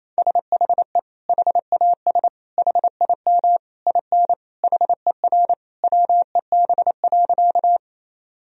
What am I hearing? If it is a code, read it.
Morse code, heard as a keyed tone, about 28 wpm, 700 Hz. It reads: SHE HAS HIM IN HER WEB.
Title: Sentences from Top 1000 Words - Review Part 4 - 28wpm